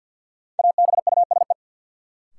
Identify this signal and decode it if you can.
Morse code, heard as a keyed tone, about 53 wpm, 700 Hz. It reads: ABUSE
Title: Sentences from Top 1300 Words - New Word Drill - 50wpm